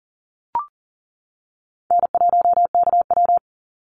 Morse code, heard as a keyed tone, about 40 wpm, 700 Hz. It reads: N1KW
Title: Easy Call Signs - 40wpm